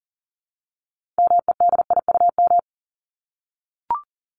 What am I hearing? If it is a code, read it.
Morse code, heard as a keyed tone, about 40 wpm, 700 Hz. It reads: MEDIUM
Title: Top 751-1000 Words 40wpm